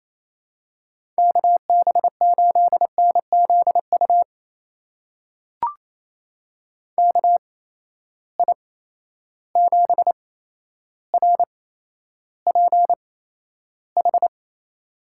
Morse code, heard as a keyed tone, about 27 wpm, spaced as for 12 wpm, 700 Hz. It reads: KB8NZU KI7RPH
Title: USA Call Signs - 6 Characters - 8x Character Spacing - 28wpm